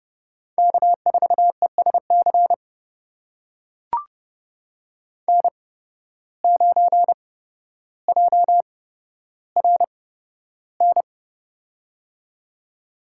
Morse code, heard as a keyed tone, about 30 wpm, spaced as for 13 wpm, 700 Hz. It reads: K4ESC N9JRN
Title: USA Call Signs - 5 Characters - 8x Character Spacing - 30wpm